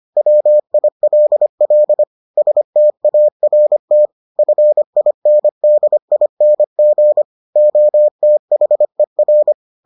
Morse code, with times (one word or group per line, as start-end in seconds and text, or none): 0.2-2.0 WILL
2.4-4.1 START
4.4-7.2 FINDING
7.6-9.5 OTHER